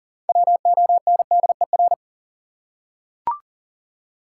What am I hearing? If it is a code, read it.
Morse code, heard as a keyed tone, about 40 wpm, 700 Hz. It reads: WONDER
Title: Six Letter Words - 40wpm